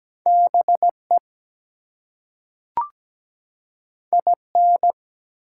Morse code, heard as a keyed tone, about 17 wpm, 700 Hz. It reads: BE IN
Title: Two Letter Words - 17wpm